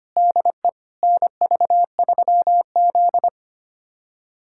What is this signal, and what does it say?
Morse code, heard as a keyed tone, about 25 wpm, 700 Hz. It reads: DE NV3Z
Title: USA Callsign DE Callsign - 4 Characters - 25wpm